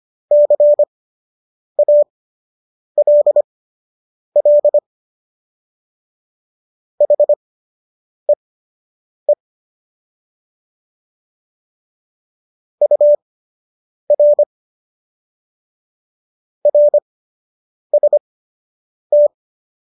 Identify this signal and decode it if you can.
Morse code, heard as a keyed tone, about 25 wpm, spaced as for 8 wpm, 590 Hz. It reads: CALL HEE UR RST
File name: qso006 08